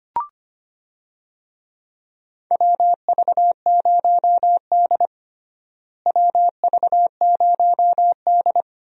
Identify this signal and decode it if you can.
Morse code, heard as a keyed tone, about 25 wpm, 700 Hz. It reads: WV0D WV0D
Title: USA Call Signs - 4 Characters - Repeated 2x - 25wpm